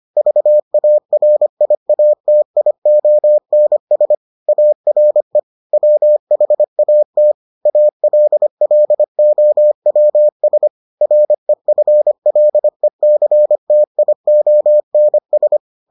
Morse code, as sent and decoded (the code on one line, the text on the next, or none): ...- .- .-. .. .- - .. --- -. ... / .- .-. / .-- .... .- - / .- .-.. .-.. --- .-- ... / .-. ..-. .-.. -.-. - .. --- -. ...
VARIATIONS ARE WHAT ALLOWS REFLECTIONS